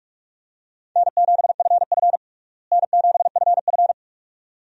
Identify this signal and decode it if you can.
Morse code, heard as a keyed tone, about 46 wpm, 700 Hz. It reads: N7FF N7FF
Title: USA Call Signs - 4 Characters - Repeated 2x - 45wpm